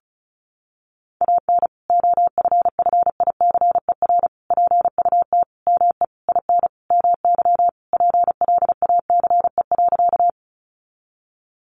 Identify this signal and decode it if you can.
Morse code, heard as a keyed tone, about 34 wpm, 700 Hz. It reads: AN OFFICER PUT ME IN MY PLACE.
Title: Sentences from Top 1200 Words 35wpm